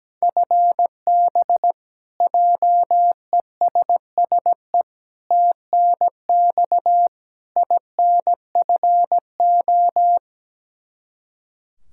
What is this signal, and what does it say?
Morse code, heard as a keyed tone, about 17 wpm, 700 Hz. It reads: FB JESSE TNX INFO
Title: K1USN SST QSO Practice 17wpm